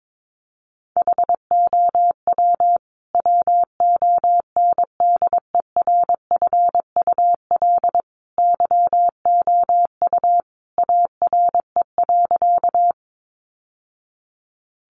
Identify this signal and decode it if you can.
Morse code, heard as a keyed tone, about 22 wpm, 700 Hz. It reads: HOW WONDERFUL YOU ARE.
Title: Sentences from Top 1100 Words 22wpm